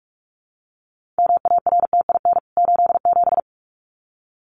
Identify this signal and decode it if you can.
Morse code, heard as a keyed tone, about 44 wpm, 700 Hz. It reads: MARTIN 87
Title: CWT Exchange 45wpm